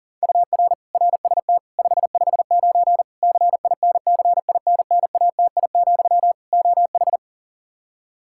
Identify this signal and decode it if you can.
Morse code, heard as a keyed tone, about 40 wpm, 700 Hz. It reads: UR RST 559 CINCINNATI, OH
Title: Practice QSO - Exchange 1 - Beginning - 40wpm